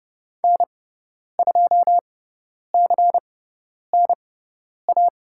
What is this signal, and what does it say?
Morse code, heard as a keyed tone, about 30 wpm, spaced as for 10 wpm, 700 Hz. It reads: N2CNA
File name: Lesson 18g - Letter m - Call Signs 30wpm